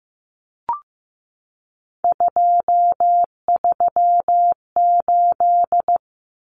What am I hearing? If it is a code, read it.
Morse code, heard as a keyed tone, about 15 wpm, 700 Hz. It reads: 238
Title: Numbers - 3-Digit 15wpm